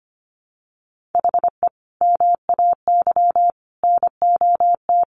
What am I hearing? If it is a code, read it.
Morse code, heard as a keyed tone, about 25 wpm, 700 Hz. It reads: HE MAY NOT
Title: Sentences from Top 1500 Words 25wpm